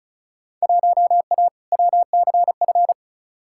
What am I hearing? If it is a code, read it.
Morse code, heard as a keyed tone, about 36 wpm, 700 Hz. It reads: 1A WCF
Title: Field Day Exchanges - 35wpm